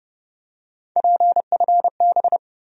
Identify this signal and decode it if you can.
Morse code, heard as a keyed tone, about 30 wpm, 700 Hz. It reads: PFB
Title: USA Call Sign Postfixes - 3 Letters 30wpm